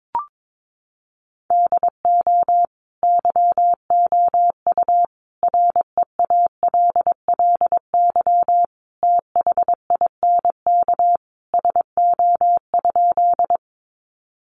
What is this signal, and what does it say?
Morse code, heard as a keyed tone, about 22 wpm, 700 Hz. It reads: DO YOU REALLY THINK SO?
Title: Sentences from Top 600 Words 22wpm